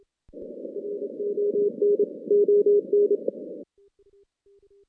Morse code, heard as a keyed tone, about 28 wpm, 410 Hz. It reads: RON ON
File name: 2016 JAN CW NAQP / 2016 Jan naqp k5ot - 0-10